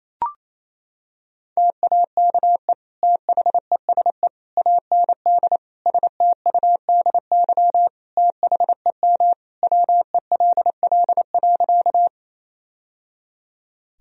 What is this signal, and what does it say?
Morse code, heard as a keyed tone, about 28 wpm, 700 Hz. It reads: TAKE THESE AND STUDY THEM WELL.
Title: Sentences from Top 1000 Words - Review Part 1 - 28wpm